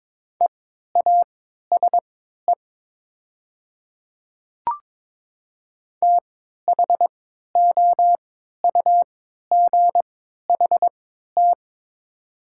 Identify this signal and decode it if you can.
Morse code, heard as a keyed tone, about 22 wpm, spaced as for 12 wpm, 700 Hz. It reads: EASE THOUGHT
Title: 3x Character Spacing - Top 300 Words - 22wpm